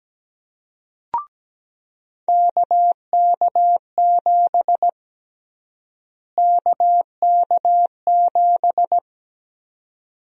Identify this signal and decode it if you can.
Morse code, heard as a keyed tone, about 17 wpm, 700 Hz. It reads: KK7 KK7
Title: USA Call Sign Prefixes - Repeated 2x - 17wpm